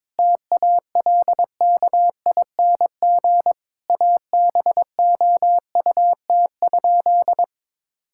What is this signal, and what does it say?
Morse code, heard as a keyed tone, about 22 wpm, 700 Hz. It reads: TALKING ABOUT?
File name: practice / 22wpm / Sentences with 2-Word Phrases - 22wpm